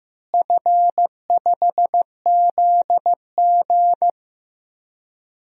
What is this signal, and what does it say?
Morse code, heard as a keyed tone, about 15 wpm, 700 Hz. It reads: F5ZG
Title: DXCC-100 Call Signs - 4 Characters - Repeated 2x - 15wpm